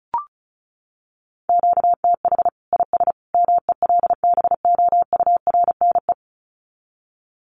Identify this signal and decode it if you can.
Morse code, heard as a keyed tone, about 34 wpm, 700 Hz. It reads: QTH IS MELBOURNE
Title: Common QSO Phrases 35wpm